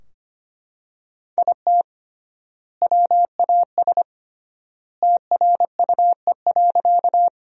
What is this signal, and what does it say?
Morse code, heard as a keyed tone, about 25 wpm, 700 Hz. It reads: IT WAS TRUE.
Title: 3x Word-Spacing - Sentences from Top 400 Words - 25wpm